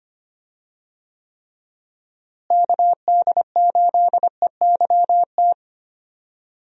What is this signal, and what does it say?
Morse code, heard as a keyed tone, about 25 wpm, 700 Hz. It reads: KD8EYT
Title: USA Call Signs - 6 Characters - Repeated 3x - 25wpm